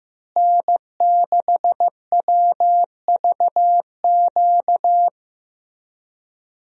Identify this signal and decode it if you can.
Morse code, heard as a keyed tone, about 15 wpm, 700 Hz. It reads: N6WVQ
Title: USA Call Signs - 5 Characters - Repeated 3x - 15wpm